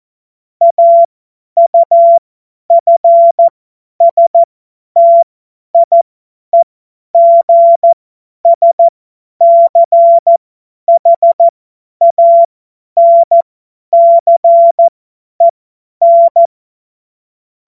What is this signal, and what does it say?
Morse code, heard as a keyed tone, about 14 wpm, spaced as for 10 wpm, 680 Hz. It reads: AUFSTIEGSCHANCEN
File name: Morsekurs / Witze / Witz 04. April 2023 50 F70